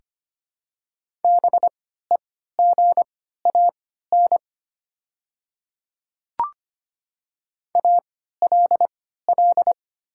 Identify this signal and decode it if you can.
Morse code, heard as a keyed tone, about 25 wpm, spaced as for 14 wpm, 700 Hz. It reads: BEGAN ALL
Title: 3x Character Spacing - Top 300 Words - 25wpm